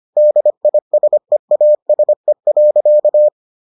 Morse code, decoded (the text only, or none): DISEASE.